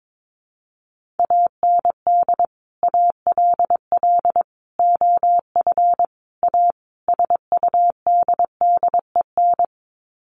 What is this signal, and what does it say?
Morse code, heard as a keyed tone, about 22 wpm, 700 Hz. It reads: AND ALL OF A SUDDEN
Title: Sets of 5 Words 22wpm